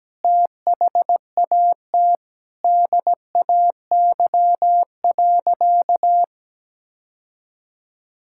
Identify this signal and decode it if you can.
Morse code, heard as a keyed tone, about 17 wpm, 700 Hz. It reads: THAT DAY.
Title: Sentences from Top 1000 Words - Review Part 5 - 17wpm